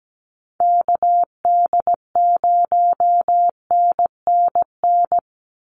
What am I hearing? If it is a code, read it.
Morse code, heard as a keyed tone, about 17 wpm, 700 Hz. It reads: KD0NNN
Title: USA Call Signs 17wpm